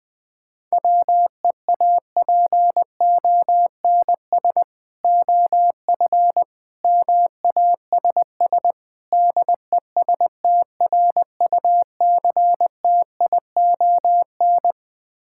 Morse code, heard as a keyed tone, about 20 wpm, 690 Hz. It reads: WEAPONS OF MASS DESTRUCTION
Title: Build A Phrase - 4-Word Phrases - 20wpm